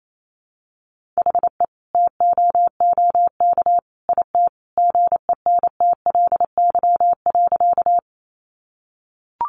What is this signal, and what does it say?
Morse code, heard as a keyed tone, about 28 wpm, 700 Hz. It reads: HE TOOK IT GENTLY.
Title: Sentences from Top 1500 Words 28wpm